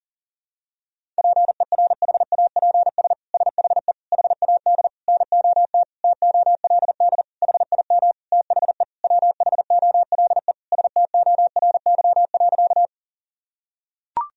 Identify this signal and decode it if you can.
Morse code, heard as a keyed tone, about 40 wpm, 700 Hz. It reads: PERHAPS SHE HAD NOT TOLD HIM THE WHOLE STORY.
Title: Sentences from Top 1000 Words - Review Part 1 - 40wpm